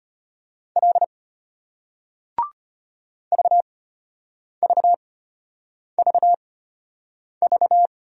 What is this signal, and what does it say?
Morse code, heard as a keyed tone, about 34 wpm, spaced as for 8 wpm, 700 Hz. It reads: L VVVV